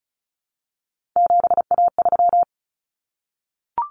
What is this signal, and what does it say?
Morse code, heard as a keyed tone, about 34 wpm, 700 Hz. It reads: 7A3
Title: Common Cut Numbers - 3-Digit 35wpm